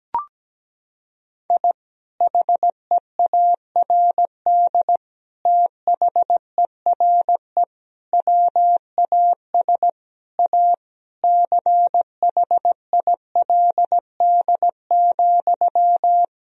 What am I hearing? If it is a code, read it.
Morse code, heard as a keyed tone, about 17 wpm, 700 Hz. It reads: I HEARD THERE WAS A CHILD,
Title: Sentences from Top 1000 Words - Review Part 2 - 17wpm